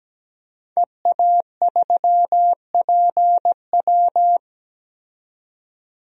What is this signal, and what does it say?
Morse code, heard as a keyed tone, about 17 wpm, 700 Hz. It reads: EA3PW
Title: DXCC-100 Call Signs - 5 Characters - 17wpm